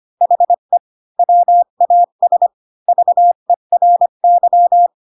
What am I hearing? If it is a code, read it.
Morse code, heard as a keyed tone, about 25 wpm, 700 Hz. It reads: HE WAS VERY
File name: Pinocchio / Pinocchio ch7 20 25 ik1hge